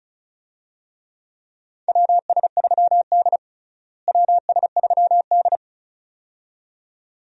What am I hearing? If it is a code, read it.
Morse code, heard as a keyed tone, about 36 wpm, 700 Hz. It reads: WS3D WS3D
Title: USA Call Signs - 4 Characters - Repeated 2x - 35wpm